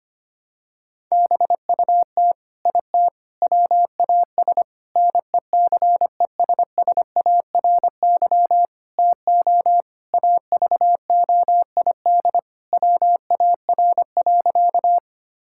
Morse code, heard as a keyed tone, about 25 wpm, 700 Hz. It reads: BUT IT WAS NECESSARY TO AVOID WAR.
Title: Sentences from Top 1000 Words - Review Part 6 - 25wpm